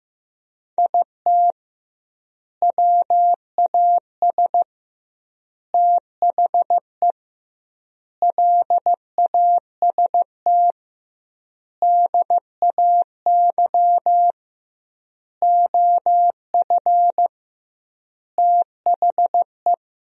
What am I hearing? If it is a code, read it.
Morse code, heard as a keyed tone, about 15 wpm, 700 Hz. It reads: IT WAS THE LAST DAY OF THE
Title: 2x Word-Spacing - Sentences from Top 300 Words - 15wpm